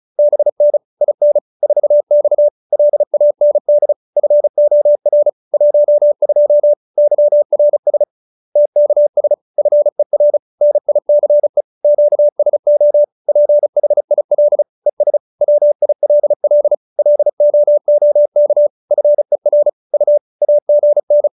Morse code, heard as a keyed tone, about 34 wpm, 590 Hz. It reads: BN IN 4X LAND FOR 12 YRS TKS FER NICE QSO PHIL ES WILL LOOK FER U AGN